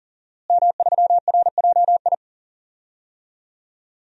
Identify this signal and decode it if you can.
Morse code, heard as a keyed tone, about 40 wpm, 700 Hz. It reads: M3RJI
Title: DXCC-100 Call Signs - 5 Characters - Repeated 2x - 40wpm